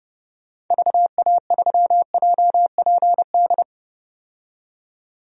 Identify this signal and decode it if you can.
Morse code, heard as a keyed tone, about 30 wpm, 700 Hz. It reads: VA3JPD
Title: DXCC-100 Call Signs - 6 Characters - 30wpm